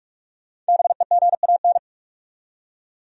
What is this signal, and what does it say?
Morse code, heard as a keyed tone, about 58 wpm, 700 Hz. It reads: BEGAN